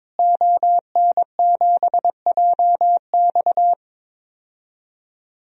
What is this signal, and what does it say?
Morse code, heard as a keyed tone, about 22 wpm, 700 Hz. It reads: ON7JX
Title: DXCC-100 Call Signs - 5 Characters - Repeated 3x - 22wpm